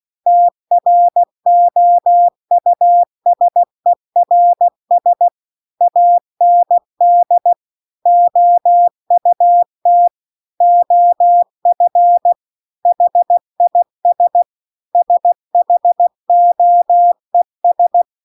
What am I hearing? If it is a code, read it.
Morse code, heard as a keyed tone, about 16 wpm, 700 Hz. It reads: TROUSERS AND OUT OF HIS SHOES
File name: CW-The-Big-Twoheaded-River-2-16WPM